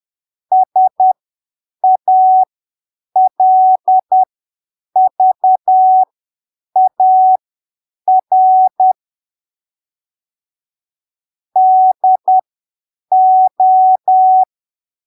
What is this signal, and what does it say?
Morse code, heard as a keyed tone, about 10 wpm, spaced as for 7 wpm, 750 Hz. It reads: SALVAR DO